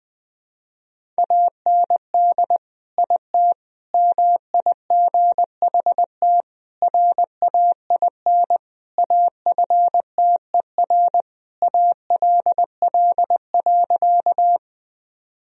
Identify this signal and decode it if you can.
Morse code, heard as a keyed tone, about 21 wpm, 700 Hz. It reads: AND IT MIGHT RAIN AFTER ALL.
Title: Sentences from Top 1000 Words - Review Part 1 - 20wpm